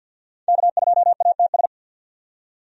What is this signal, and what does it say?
Morse code, heard as a keyed tone, about 47 wpm, 700 Hz. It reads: K2ATS
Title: USA Call Signs - 5 Characters - 50wpm